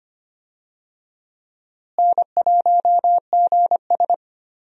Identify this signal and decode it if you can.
Morse code, heard as a keyed tone, about 25 wpm, 700 Hz. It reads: N1GS
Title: USA Call Signs - 4 Characters - Repeated 2x - 25wpm